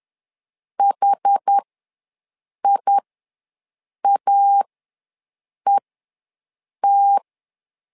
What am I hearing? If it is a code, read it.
Morse code, heard as a keyed tone, about 10 wpm, spaced as for 6 wpm, 800 Hz. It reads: HIAET
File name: LEZIONI / 2 UHDN / 2 UH ws 10wpm 2029